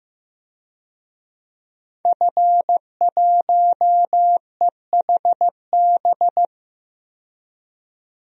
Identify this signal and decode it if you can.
Morse code, heard as a keyed tone, about 15 wpm, 700 Hz. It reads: F1EHB
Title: DXCC-100 Call Signs - 5 Characters - 15wpm